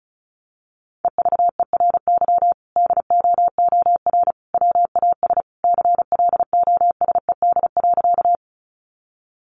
Morse code, heard as a keyed tone, about 34 wpm, 700 Hz. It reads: EVERY DOOR WAS CLOSED.
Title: Sentences from Top 1400 Words 35wpm